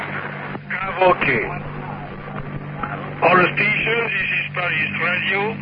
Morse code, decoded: none